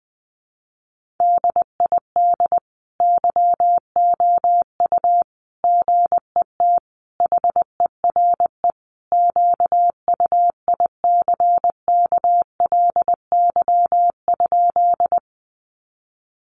Morse code, heard as a keyed tone, about 20 wpm, 700 Hz. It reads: DID YOU GET HERE QUICKLY?
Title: Sentences from Top 700 Words 20wpm